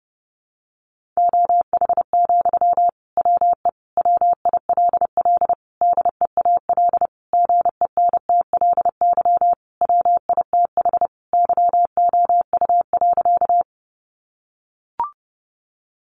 Morse code, heard as a keyed tone, about 30 wpm, 700 Hz. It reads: OH, WE WILL DEAL GENTLY WITH YOU.
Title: Sentences from Top 1500 Words 30wpm